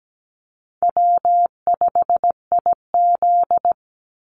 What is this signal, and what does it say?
Morse code, heard as a keyed tone, about 17 wpm, 700 Hz. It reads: W5IZ
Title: Call Signs Repeated 3x - 17wpm